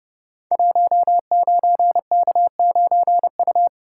Morse code, heard as a keyed tone, about 30 wpm, 700 Hz. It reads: 19K9U
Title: Code Groups - 5 Letter:Numbers - 30wpm